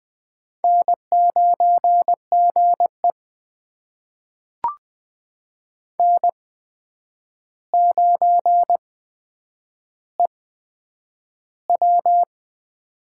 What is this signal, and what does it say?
Morse code, heard as a keyed tone, about 20 wpm, spaced as for 9 wpm, 700 Hz. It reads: N9GE N9EW
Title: USA Call Signs - 4 Characters - 8x Character Spacing - 20wpm